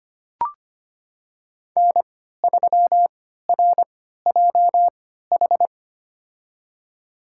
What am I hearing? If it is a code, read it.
Morse code, heard as a keyed tone, about 25 wpm, spaced as for 14 wpm, 700 Hz. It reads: N3RJH
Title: USA Call Signs - 5 Characters - 3x Character Spacing - 25wpm